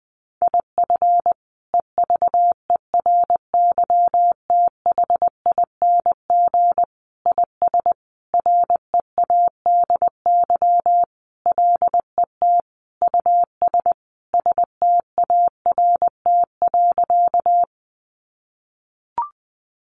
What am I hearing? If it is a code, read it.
Morse code, heard as a keyed tone, about 20 wpm, 700 Hz. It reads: IF EVERYTHING IS READY LET US START.
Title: Sentences from Top 700 Words 20wpm